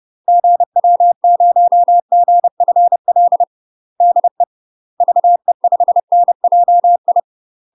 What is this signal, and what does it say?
Morse code, heard as a keyed tone, about 30 wpm, 700 Hz. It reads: GW0GFL DE VE5NJI
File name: UK MASTER QSOs TRAINING / 30WPM / UK MASTER QSOs 0009